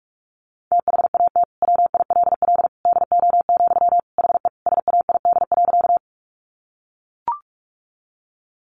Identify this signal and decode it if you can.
Morse code, heard as a keyed tone, about 44 wpm, 700 Hz. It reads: THAT WILL DO, HE SAID.
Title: Sentences from Top 100 Words 45wpm